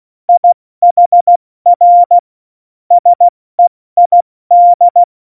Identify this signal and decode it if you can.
Morse code, heard as a keyed tone, about 16 wpm, 700 Hz. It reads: IHR SEID
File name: Morsekurs / Witze / Witz 01. Dezember 2020, 70 F80